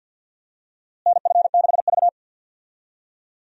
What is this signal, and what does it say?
Morse code, heard as a keyed tone, about 45 wpm, 700 Hz. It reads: NF6V
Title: USA Call Signs - 4 Characters - Repeated 3x - 50wpm